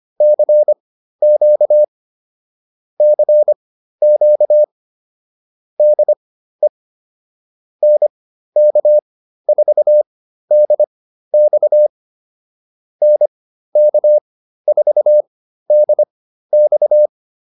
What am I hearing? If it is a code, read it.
Morse code, heard as a keyed tone, about 25 wpm, spaced as for 13 wpm, 600 Hz. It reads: CQ CQ DE NK4DX NK4DX